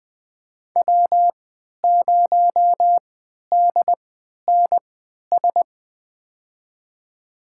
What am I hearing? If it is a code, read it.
Morse code, heard as a keyed tone, about 20 wpm, spaced as for 11 wpm, 700 Hz. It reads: W0DNS
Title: USA Call Signs - 5 Characters - 3x Character Spacing - 20wpm